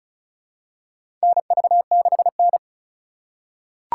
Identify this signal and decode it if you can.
Morse code, heard as a keyed tone, about 36 wpm, 700 Hz. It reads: NV6N
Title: USA Call Signs - 4 Characters - 3x Character Spacing - 35wpm